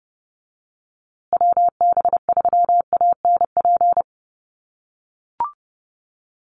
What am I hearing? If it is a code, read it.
Morse code, heard as a keyed tone, about 30 wpm, 700 Hz. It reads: WB3ANP
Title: USA Call Signs Repeated 2x 30wpm